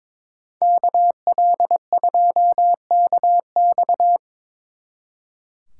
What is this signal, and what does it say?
Morse code, heard as a keyed tone, about 25 wpm, 700 Hz. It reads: KL2KX
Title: Speed Racing - USA 2x2 Call Signs 22wpm